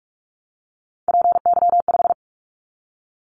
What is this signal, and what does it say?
Morse code, heard as a keyed tone, about 46 wpm, 700 Hz. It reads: PY5
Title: Callsign Prefixes for Top 50 DXCC - 45wpm